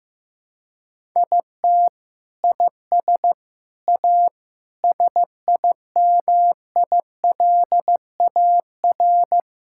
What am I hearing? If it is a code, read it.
Morse code, heard as a keyed tone, about 16 wpm, 700 Hz. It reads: IT IS A SIMILAR
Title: Sentences from Top 1000 Words - Review Part 5 - 15wpm